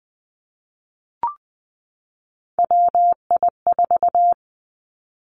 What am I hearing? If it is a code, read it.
Morse code, heard as a keyed tone, about 20 wpm, 700 Hz. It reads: WI4
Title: Three Challenging Letter-Number Combinations 20wpm